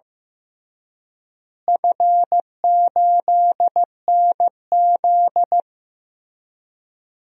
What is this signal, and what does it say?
Morse code, heard as a keyed tone, about 15 wpm, 700 Hz. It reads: F8NZ
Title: DXCC-100 Call Signs - 4 Characters - Repeated 2x - 15wpm